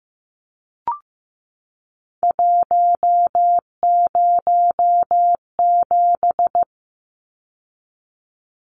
Morse code, heard as a keyed tone, about 15 wpm, 700 Hz. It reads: 107
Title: Numbers - 3-Digit 15wpm